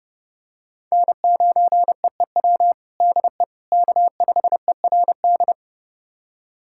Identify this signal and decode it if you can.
Morse code, heard as a keyed tone, about 30 wpm, 700 Hz. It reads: N9EEW DE K5ERD
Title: USA Callsign DE Callsign - 5 Characters - 30wpm